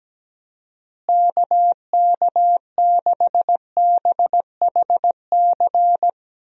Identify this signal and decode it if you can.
Morse code, heard as a keyed tone, about 17 wpm, 700 Hz. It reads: KK6BHC
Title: USA Call Signs - 6 Characters - Repeated 2x - 17wpm